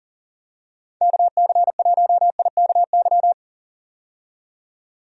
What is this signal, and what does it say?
Morse code, heard as a keyed tone, about 40 wpm, 700 Hz. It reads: KC1IKY
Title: USA Call Signs - 6 Characters - 40wpm